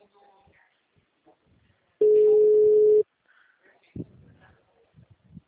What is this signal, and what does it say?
Ringback tone (Europe, 425 Hz).
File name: out- -2006- - - .73314